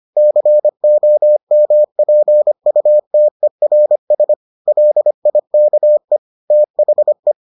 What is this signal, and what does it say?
Morse code, heard as a keyed tone, about 25 wpm, 600 Hz. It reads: COMPUTERS LIKE THE